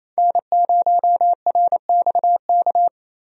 Morse code, heard as a keyed tone, about 28 wpm, 700 Hz. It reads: N0RXK
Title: USA Call Signs - 5 Characters - 28wpm